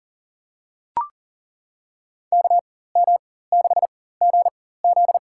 Morse code, heard as a keyed tone, about 40 wpm, spaced as for 19 wpm, 700 Hz. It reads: KM6GZ
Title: USA Call Signs - 4x Character Spacing - 40wpm